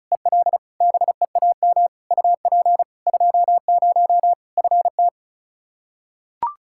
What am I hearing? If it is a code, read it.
Morse code, heard as a keyed tone, about 36 wpm, 700 Hz. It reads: EL BEAM UP 20 FT
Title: Practice QSO - Exchange 2 - Info - 35wpm